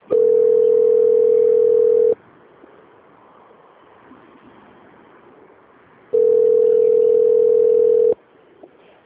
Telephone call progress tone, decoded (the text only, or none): ringback tone